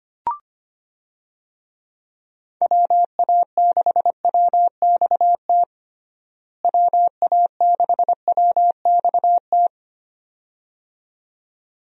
Morse code, heard as a keyed tone, about 25 wpm, 700 Hz. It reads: WA6WXT WA6WXT
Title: USA Call Signs - 6 Characters - Repeated 2x - 25wpm